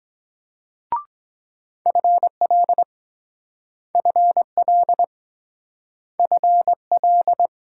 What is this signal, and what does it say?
Morse code, heard as a keyed tone, about 22 wpm, 700 Hz. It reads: FL FL FL